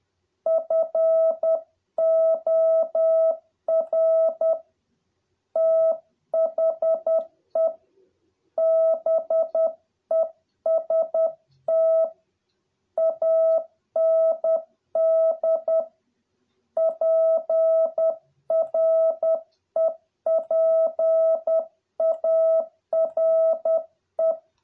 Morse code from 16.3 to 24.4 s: PREPARE